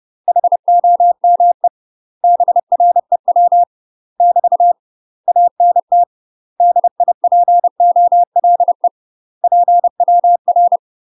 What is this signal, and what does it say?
Morse code, heard as a keyed tone, about 30 wpm, 700 Hz. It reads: HOME BREW <BT> ANT DIPOLE PWR
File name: UK MASTER QSOs TRAINING / 30WPM / UK MASTER QSOs 0048